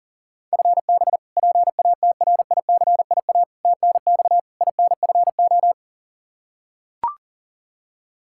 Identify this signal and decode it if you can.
Morse code, heard as a keyed tone, about 37 wpm, 700 Hz. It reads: FB PATRICIA TNX INFO
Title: Practice QSO - Exchange 3 - More Info - 40wpm